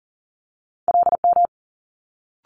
Morse code, heard as a keyed tone, about 40 wpm, 700 Hz. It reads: LM
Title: Multi-letter Phonemes Quick Fire 40wpm